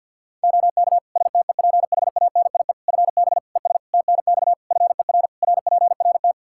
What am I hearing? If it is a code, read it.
Morse code, heard as a keyed tone, about 53 wpm, 700 Hz. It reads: OK STEPHANIE FB ES TNX FER RPRT
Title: Practice QSO - Exchange 2 - Info - 50wpm